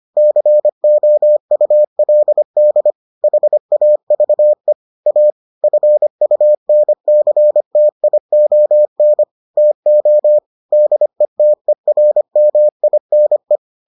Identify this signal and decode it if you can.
Morse code, heard as a keyed tone, about 25 wpm, 600 Hz. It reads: COULD HAVE A FUNCTION TO DETERMINE